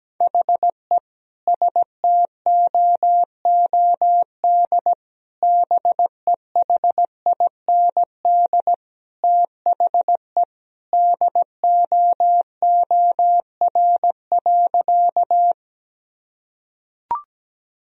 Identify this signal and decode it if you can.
Morse code, heard as a keyed tone, about 17 wpm, 700 Hz. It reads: HE STOOD BEHIND THE DOOR.
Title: Sentences from Top 1000 Words - Review Part 2 - 17wpm